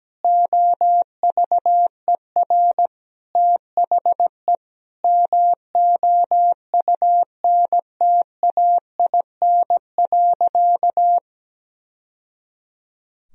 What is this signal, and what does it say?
Morse code, heard as a keyed tone, about 17 wpm, 700 Hz. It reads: OVER THE MOUNTAIN.
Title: Sentences from Top 1000 Words - Review Part 1 - 17wpm